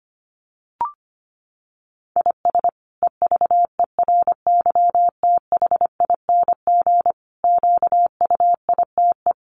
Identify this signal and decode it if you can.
Morse code, heard as a keyed tone, about 25 wpm, 700 Hz. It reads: IS EVERYTHING QUITE
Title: Sentences from Top 700 Words 25wpm